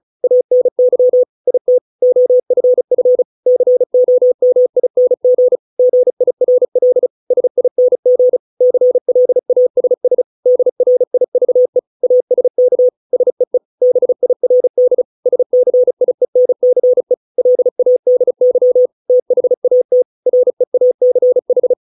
Morse code, heard as a keyed tone, about 34 wpm, 500 Hz. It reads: ANY IT OFF COMING GIRL SING CLASS DRIVE ASK SEE BIRD SCIENCE LADY THAT REACH